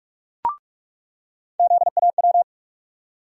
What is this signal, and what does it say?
Morse code, heard as a keyed tone, about 46 wpm, 700 Hz. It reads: ZAW
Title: USA Call Sign Postfixes - 3 Letters 45wpm